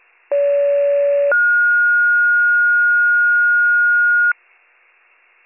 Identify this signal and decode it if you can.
Two-tone page, heard as a one-second tone, then a three-second tone, about 570 then 1430 Hz. It is heard from 0.3 to 4.3 s.